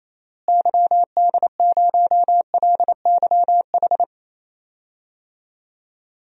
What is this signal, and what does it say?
Morse code, heard as a keyed tone, about 28 wpm, 700 Hz. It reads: YD0LYH
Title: DXCC-100 Call Signs - 6 Characters - Repeated 2x - 28wpm